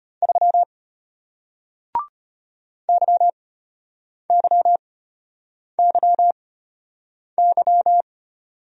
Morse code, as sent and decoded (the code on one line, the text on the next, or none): ...-- / -.-- -.-- -.-- -.--
3 YYYY